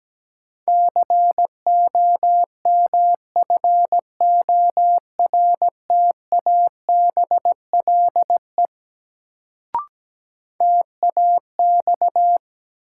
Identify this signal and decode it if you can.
Morse code, heard as a keyed tone, about 17 wpm, 700 Hz. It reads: COMFORTABLE TAX
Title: Sentences from Top 1500 Words - New Word Drill Alt - 17wpm